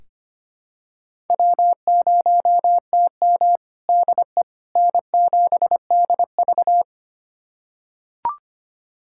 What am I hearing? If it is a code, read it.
Morse code, heard as a keyed tone, about 25 wpm, 700 Hz. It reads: W0TM DE N7DV
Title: USA Callsign DE Callsign - 4 Characters - 25wpm